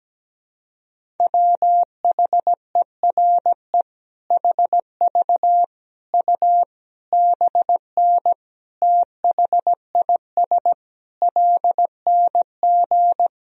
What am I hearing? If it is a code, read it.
Morse code, heard as a keyed tone, about 17 wpm, 700 Hz. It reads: WHERE HV U BN THIS LNG
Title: Sentences from Top 300 Words and Abbreviations 17wpm